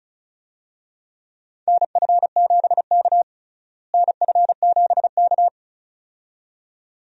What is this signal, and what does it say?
Morse code, heard as a keyed tone, about 34 wpm, 700 Hz. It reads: NF7K NF7K
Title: USA Call Signs - 4 Characters - Repeated 2x - 35wpm